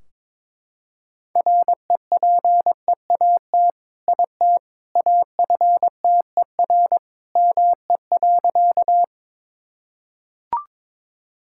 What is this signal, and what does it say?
Morse code, heard as a keyed tone, about 22 wpm, 700 Hz. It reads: REPEAT IT AFTER ME.
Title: Sentences from Top 1600 Words 22wpm